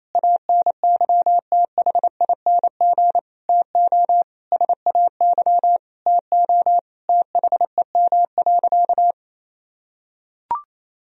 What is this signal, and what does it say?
Morse code, heard as a keyed tone, about 28 wpm, 700 Hz. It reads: ANYTHING TO SAY TO THEM.
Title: Sentences from Top 1000 Words - Review Part 1 - 28wpm